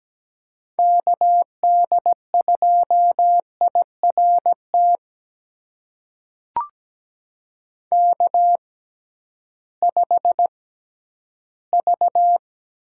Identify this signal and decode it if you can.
Morse code, heard as a keyed tone, about 17 wpm, spaced as for 11 wpm, 700 Hz. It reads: KD2IRT K5V
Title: USA Call Signs - 6x Character Spacing - 17wpm